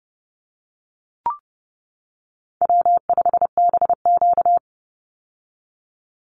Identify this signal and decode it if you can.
Morse code, heard as a keyed tone, about 28 wpm, 700 Hz. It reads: W5BQ
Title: Easy Call Signs - 30wpm